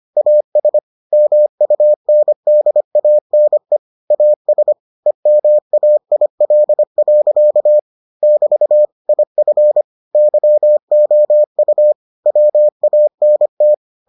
Morse code, heard as a keyed tone, about 25 wpm, 600 Hz. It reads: AS MUNDANE AS EMAIL. <BT> IF YOU WANT